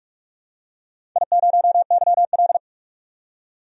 Morse code, heard as a keyed tone, about 44 wpm, 700 Hz. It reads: I0YL
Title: DXCC-100 Call Signs - 4 Characters - 45wpm